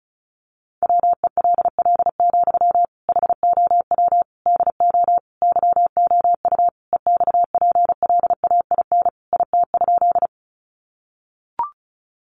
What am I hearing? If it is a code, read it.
Morse code, heard as a keyed tone, about 34 wpm, 700 Hz. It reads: WELL, HOW DO YOU EXPLAIN IT?
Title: Sentences from Top 1100 Words 35wpm